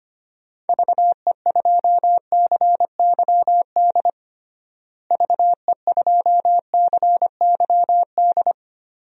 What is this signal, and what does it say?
Morse code, heard as a keyed tone, about 25 wpm, 700 Hz. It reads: VE2CYD VE2CYD
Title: DXCC-100 Call Signs - 6 Characters - Repeated 2x - 25wpm